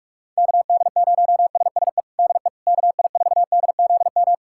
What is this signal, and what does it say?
Morse code, heard as a keyed tone, about 46 wpm, 700 Hz. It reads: KD0SSE DE KI4DZM
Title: USA Callsign DE Callsign 45wpm